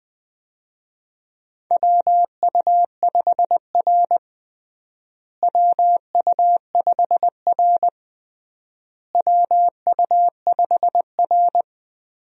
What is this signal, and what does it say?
Morse code, heard as a keyed tone, about 20 wpm, 700 Hz. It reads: WU5R WU5R WU5R
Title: USA Call Signs - 4 Characters - Repeated 3x - 20wpm